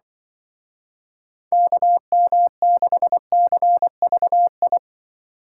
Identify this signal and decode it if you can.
Morse code, heard as a keyed tone, about 24 wpm, 700 Hz. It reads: KM6CVI
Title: Speed Racing - USA 2x3 Call Signs 20wpm